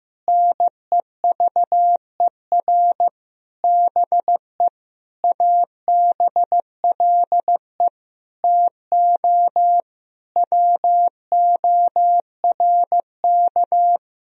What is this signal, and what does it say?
Morse code, heard as a keyed tone, about 15 wpm, 700 Hz. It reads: NEVER BE ABLE TO WORK
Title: Sentences from Top 1000 Words - Review Part 2 - 15wpm